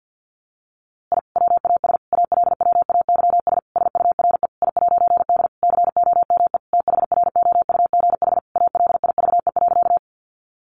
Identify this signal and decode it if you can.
Morse code, heard as a keyed tone, about 50 wpm, 700 Hz. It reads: I WAS ALWAYS SURE I'D COME THROUGH ALIVE.
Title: A Sound Of Thunder 50wpm